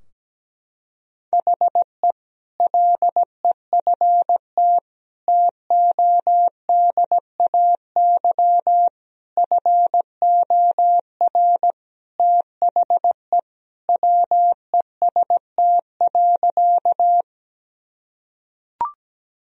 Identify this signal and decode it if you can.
Morse code, heard as a keyed tone, about 17 wpm, 700 Hz. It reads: HE LEFT TODAY FOR THE WEST.
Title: Sentences from Top 1000 Words - Review Part 2 - 17wpm